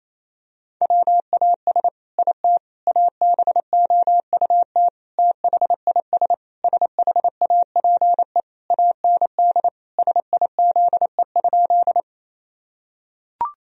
Morse code, heard as a keyed tone, about 27 wpm, 700 Hz. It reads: WAS IT ABOUT THIS SHAPE AND SIZE?
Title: Sentences from Top 1000 Words - Review Part 3 - 28wpm